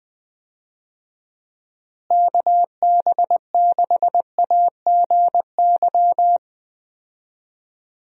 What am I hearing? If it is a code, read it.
Morse code, heard as a keyed tone, about 20 wpm, 700 Hz. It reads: KB6AGY
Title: USA Call Signs - 6 Characters - Repeated 3x - 20wpm